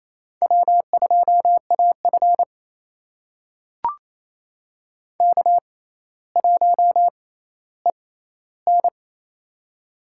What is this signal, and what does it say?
Morse code, heard as a keyed tone, about 28 wpm, spaced as for 14 wpm, 700 Hz. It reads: W2AF K1EN
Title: USA Call Signs - 4 Characters - 6x Character Spacing - 28wpm